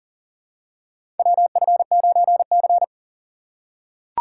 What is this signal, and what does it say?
Morse code, heard as a keyed tone, about 40 wpm, 700 Hz. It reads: WF9C
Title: USA Call Signs - 4x Character Spacing - 40wpm